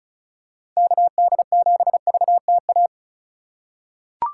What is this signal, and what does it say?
Morse code, heard as a keyed tone, about 36 wpm, 700 Hz. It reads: KD7VTA
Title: USA Call Signs - 6 Characters - 4x Character Spacing - 35wpm